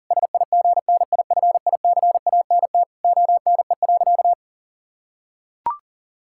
Morse code, heard as a keyed tone, about 40 wpm, 690 Hz. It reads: SIGNIFICANT ONE.
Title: Sentences from Top 1000 Words - Review Part 6 - 40wpm